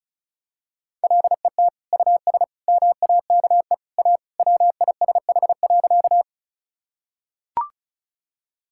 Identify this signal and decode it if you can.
Morse code, heard as a keyed tone, about 34 wpm, 700 Hz. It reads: LET US MAKE A WISH.
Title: Sentences from Top 1600 Words 35wpm